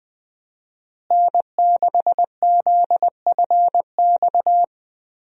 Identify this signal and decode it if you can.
Morse code, heard as a keyed tone, about 20 wpm, 700 Hz. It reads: N6ZFX
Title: USA Call Signs - 5 Characters - Repeated 3x - 20wpm